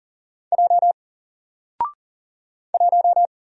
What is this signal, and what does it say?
Morse code, heard as a keyed tone, about 40 wpm, 700 Hz. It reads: J 1
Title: Single Letter-Number Alt 40wpm